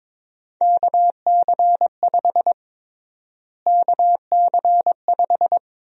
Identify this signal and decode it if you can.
Morse code, heard as a keyed tone, about 23 wpm, 700 Hz. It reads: KC5 KC5
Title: USA Call Sign Prefixes - Repeated 2x - 22wpm